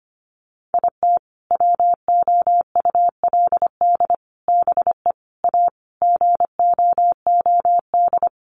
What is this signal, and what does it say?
Morse code, heard as a keyed tone, about 25 wpm, 700 Hz. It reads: IT WOULD BE A GOOD
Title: Sentences from Top 1200 Words 25wpm